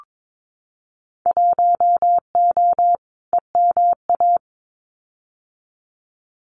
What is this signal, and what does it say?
Morse code, heard as a keyed tone, about 22 wpm, 700 Hz. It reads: 1O EMA
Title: Winter Field Day Exchanges 22wpm